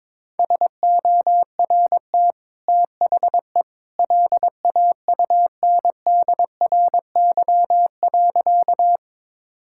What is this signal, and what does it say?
Morse code, heard as a keyed tone, about 22 wpm, 690 Hz. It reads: SORT THE LAUNDRY.
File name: practice / 22wpm / Easy Sentences with Hints - 22wpm